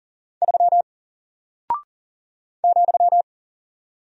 Morse code, heard as a keyed tone, about 40 wpm, 700 Hz. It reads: 3 ,
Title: Single Character (Letter:Number:Punctuation:Prosign) Alt 40wpm